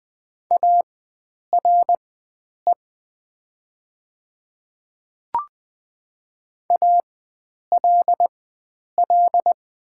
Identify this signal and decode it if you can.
Morse code, heard as a keyed tone, about 20 wpm, spaced as for 9 wpm, 700 Hz. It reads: ARE ALL
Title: 4x Character Spacing - Top 100 Words - 20wpm